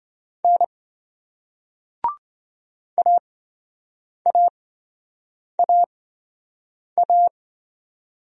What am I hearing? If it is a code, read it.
Morse code, heard as a keyed tone, about 27 wpm, spaced as for 7 wpm, 700 Hz. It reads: N AAAA